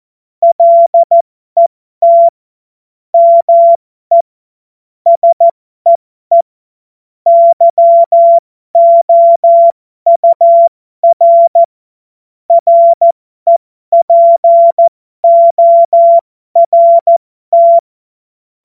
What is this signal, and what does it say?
Morse code, heard as a keyed tone, about 14 wpm, spaced as for 12 wpm, 680 Hz. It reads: LET ME SEE YOUR REPORT
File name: Morsekurs / Witze